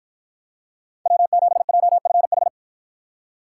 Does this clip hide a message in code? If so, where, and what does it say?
Yes: Morse code from 1.1 to 2.5 s, heard as a keyed tone, about 53 wpm, 700 Hz. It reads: W7JFH